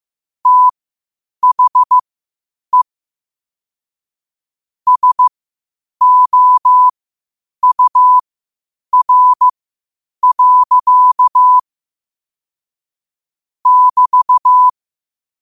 Morse code, heard as a keyed tone, about 15 wpm, spaced as for 8 wpm, 1000 Hz. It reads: THE SOUR. <BT>
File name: qotdfiles / 8wpm